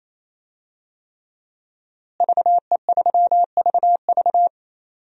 Morse code, heard as a keyed tone, about 28 wpm, 700 Hz. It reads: VE3VV
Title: DXCC-100 Call Signs - 5 Characters - Repeated 3x - 28wpm